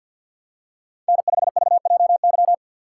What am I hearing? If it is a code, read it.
Morse code, heard as a keyed tone, about 47 wpm, 700 Hz. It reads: N5VJY
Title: USA Call Signs - 5 Characters - 4x Character Spacing - 50wpm